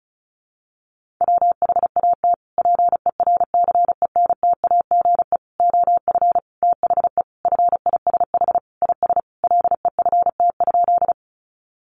Morse code, heard as a keyed tone, about 34 wpm, 700 Hz. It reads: WHAT PERCENTAGE OF THE FISH IS LEFT?
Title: Sentences from Top 1300 Words 35wpm